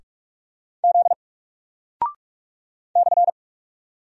Morse code, heard as a keyed tone, about 46 wpm, 700 Hz. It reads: Z /
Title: Single Character (Letter:Number:Punctuation:Prosign) Alt 45wpm